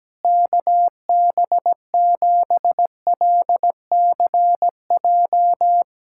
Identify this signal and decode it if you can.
Morse code, heard as a keyed tone, about 17 wpm, 700 Hz. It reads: KB7LCJ
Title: USA Callsign DE Callsign 17wpm